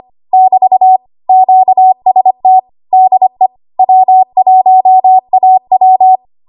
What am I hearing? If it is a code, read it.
Morse code, heard as a keyed tone, about 25 wpm, 750 Hz. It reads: <BT> QST DE W1AW
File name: Archive / 25 WPM / 25WPM